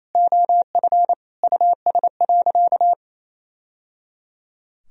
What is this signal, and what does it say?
Morse code, heard as a keyed tone, about 27 wpm, 700 Hz. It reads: OF US.
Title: Sentences from Top 1000 Words - Review Part 4 - 28wpm